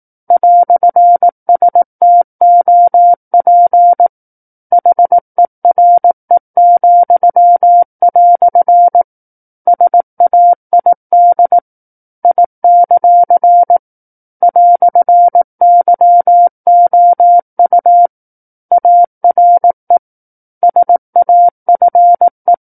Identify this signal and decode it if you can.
Morse code, heard as a keyed tone, about 18 wpm, 700 Hz. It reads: "STOP HERE," SAID I; "YOU ARE SAFE
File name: 18 WPM